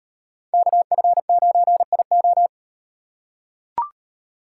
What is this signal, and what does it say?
Morse code, heard as a keyed tone, about 35 wpm, 700 Hz. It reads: KF9IO